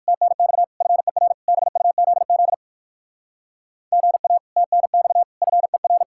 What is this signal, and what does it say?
Morse code, heard as a keyed tone, about 62 wpm, 700 Hz. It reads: TNX FER BUZZ GA TNX FER